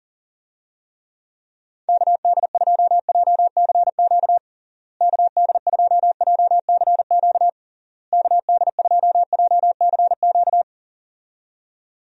Morse code, heard as a keyed tone, about 40 wpm, 700 Hz. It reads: KD2JCQ KD2JCQ KD2JCQ
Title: USA Call Signs - 6 Characters - Repeated 3x - 40wpm